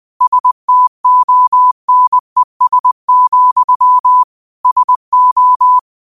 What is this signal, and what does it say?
Morse code, heard as a keyed tone, about 20 wpm, 1000 Hz. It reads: STONES, SO
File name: qotdfiles / 20wpm